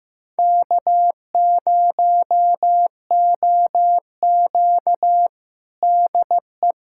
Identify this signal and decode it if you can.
Morse code, heard as a keyed tone, about 15 wpm, 700 Hz. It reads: K0OQ DE
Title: USA Callsign DE Callsign - 4 Characters - 15wpm